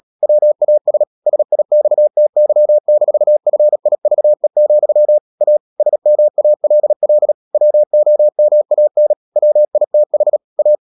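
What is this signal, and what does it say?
Morse code, heard as a keyed tone, about 37 wpm, 600 Hz. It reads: WAS SIXTY-FIVE, A SMALL WOMAN WITH A
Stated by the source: Morse code practice at 35–39 wpm sending English text of Ken Follett's novel Never, from CW med Gnister.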